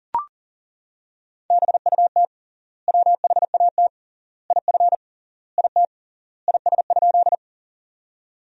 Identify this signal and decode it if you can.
Morse code, heard as a keyed tone, about 40 wpm, 700 Hz. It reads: BUT WHAT IF IT IS?
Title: 3x Word-Spacing - Sentences from Top 100 Words - 40wpm